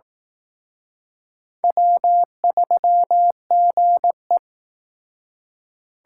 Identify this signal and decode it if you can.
Morse code, heard as a keyed tone, about 18 wpm, 700 Hz. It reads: W3GE
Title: Speed Racing - USA 1x2 Call Signs 15wpm